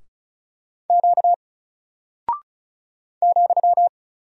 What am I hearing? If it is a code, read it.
Morse code, heard as a keyed tone, about 35 wpm, 700 Hz. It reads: Q ,
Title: Single Character (Letter:Number:Punctuation:Prosign) 35wpm